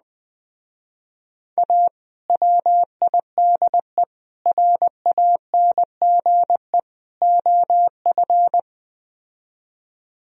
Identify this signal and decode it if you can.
Morse code, heard as a keyed tone, about 20 wpm, 700 Hz. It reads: A WIDE RANGE OF
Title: Speed-Racing - Sets of 4 Words 17wpm